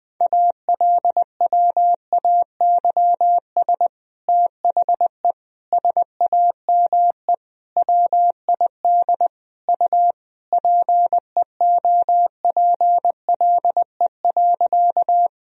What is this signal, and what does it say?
Morse code, heard as a keyed tone, about 21 wpm, 700 Hz. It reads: ALWAYS THE SAME WID U PEOPLE.
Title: Sentences from Top 300 Words and Abbreviations 20wpm